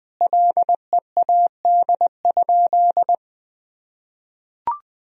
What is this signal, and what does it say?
Morse code, heard as a keyed tone, about 20 wpm, 700 Hz. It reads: LEAD?
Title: Sentences from Top 1000 Words - Review Part 2 - 20wpm